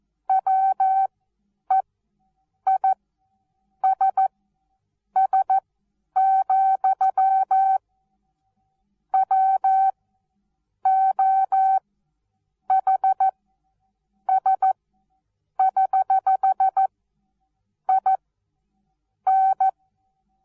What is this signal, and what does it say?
Morse code, heard as a keyed tone, about 14 wpm, spaced as for 7 wpm, 770 Hz. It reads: WEISS, WOHS<HH>IN
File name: Onlinelehrgang / morsen / DJ4UF Morsen19